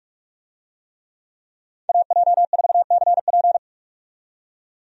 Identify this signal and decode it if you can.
Morse code, heard as a keyed tone, about 47 wpm, 700 Hz. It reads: AJ4CP
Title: USA Call Signs - 5 Characters - 45wpm